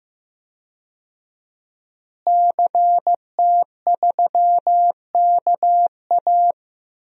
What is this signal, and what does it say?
Morse code, heard as a keyed tone, about 15 wpm, 700 Hz. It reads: CT3KA
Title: DXCC-100 Call Signs - 5 Characters - Repeated 3x - 15wpm